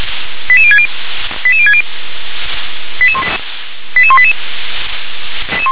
Five-tone selective call (ZVEI), heard as quick runs of five tones, tones of about 70 ms. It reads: 88060 88060 88188